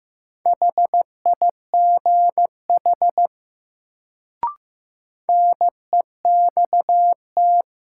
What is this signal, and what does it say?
Morse code, heard as a keyed tone, about 15 wpm, 700 Hz. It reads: HIGH NEXT
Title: Sets of 2 Words - New Word Drill - 15wpm